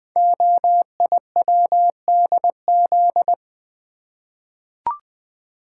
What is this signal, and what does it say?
Morse code, heard as a keyed tone, about 20 wpm, 700 Hz. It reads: OIWDZ
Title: Code Groups - 5 Letters - 20wpm